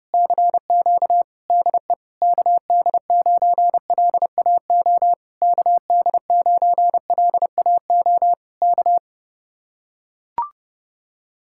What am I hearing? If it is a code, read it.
Morse code, heard as a keyed tone, about 30 wpm, 700 Hz. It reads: CQ DE KD9LAO KD9LAO K